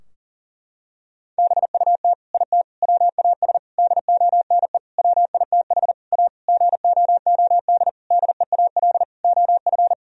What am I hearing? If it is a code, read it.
Morse code, heard as a keyed tone, about 40 wpm, 700 Hz. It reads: BUT IT WAS DONE WITH A GOOD DEAL OF
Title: Sentences from Top 1000 Words - Review Part 4 - 40wpm